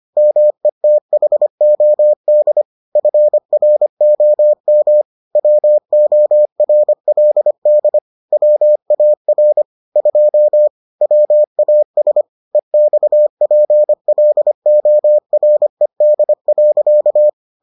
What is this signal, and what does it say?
Morse code, heard as a keyed tone, about 25 wpm, 600 Hz. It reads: METHOD FROM WORLD WAR 2 WAS EXPLORED.